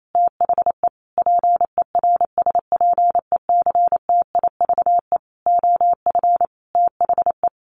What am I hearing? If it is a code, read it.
Morse code, heard as a keyed tone, about 28 wpm, 700 Hz. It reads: THE PERSPECTIVE OF THE
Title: Sets of 5 Words Encore2 28wpm